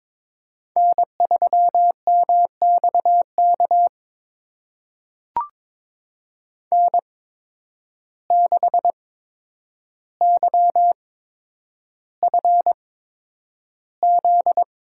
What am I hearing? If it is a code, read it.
Morse code, heard as a keyed tone, about 22 wpm, spaced as for 9 wpm, 700 Hz. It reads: N3MXK N6YFZ
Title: USA Call Signs - 5 Characters - 8x Character Spacing - 22wpm